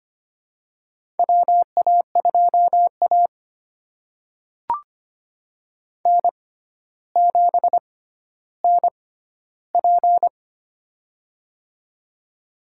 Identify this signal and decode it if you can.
Morse code, heard as a keyed tone, about 25 wpm, spaced as for 13 wpm, 700 Hz. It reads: WA2A N7NP
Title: USA Call Signs - 4 Characters - 6x Character Spacing - 25wpm